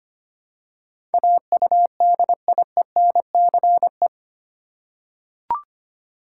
Morse code, heard as a keyed tone, about 25 wpm, 700 Hz. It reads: AUDIENCE
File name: 8 Letter Words - First 3 Letters Given - 25wpm